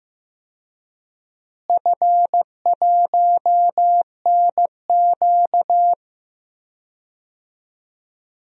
Morse code, heard as a keyed tone, about 15 wpm, 700 Hz. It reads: F1NQ
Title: DXCC-100 Call Signs - 4 Characters - Repeated 2x - 15wpm